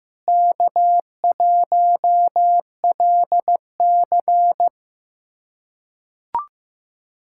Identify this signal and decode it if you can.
Morse code, heard as a keyed tone, about 15 wpm, 700 Hz. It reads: K1LC
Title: USA Callsign DE Callsign - 4 Characters - 15wpm